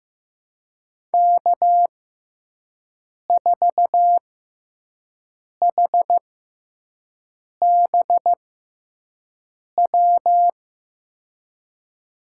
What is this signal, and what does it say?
Morse code, heard as a keyed tone, about 15 wpm, spaced as for 5 wpm, 700 Hz. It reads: K4HBW
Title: USA Call Signs - 5 Characters - 6x Character Spacing - 15wpm